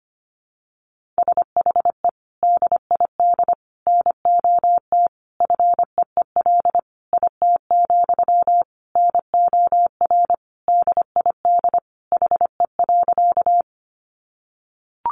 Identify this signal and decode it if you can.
Morse code, heard as a keyed tone, about 25 wpm, 700 Hz. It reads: SHE DID NOT FEEL IT, NOR DID HE.
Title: Sentences from Top 1500 Words 25wpm